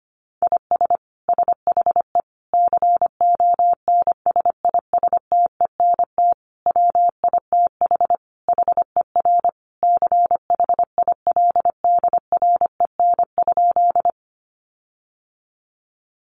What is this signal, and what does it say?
Morse code, heard as a keyed tone, about 25 wpm, 700 Hz. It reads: IS SHE CONSISTENT WITH HER CHILDREN?
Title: Sentences from Top 1500 Words 25wpm